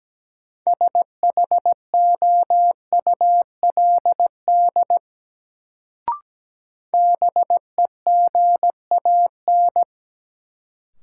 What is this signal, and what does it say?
Morse code, heard as a keyed tone, about 17 wpm, 700 Hz. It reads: SHOULD BEGAN
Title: Sentences from Top 300 Words - New Word Drill - 17wpm